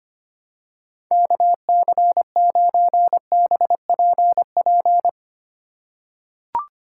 Morse code, heard as a keyed tone, about 25 wpm, 700 Hz. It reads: KC9BPP
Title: USA Call Signs - 6 Characters - 2x Character Spacing - 25wpm